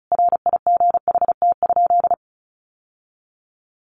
Morse code, heard as a keyed tone, about 34 wpm, 700 Hz. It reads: RIGHT?
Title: Sentences from Top 800 Words 35wpm